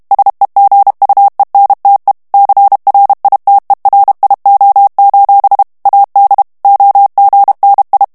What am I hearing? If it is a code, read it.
Morse code, heard as a keyed tone, about 32 wpm, 800 Hz. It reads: SEGUENTE CRITERIO: AD OGNI
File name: ITALIEN texte clair 2 vitesse 30 mots minute